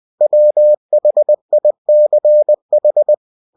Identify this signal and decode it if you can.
Morse code, heard as a keyed tone, about 20 wpm, 600 Hz. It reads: WHICH
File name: cw-books / winnie-the-pooh / winnie-20wpm-page-0000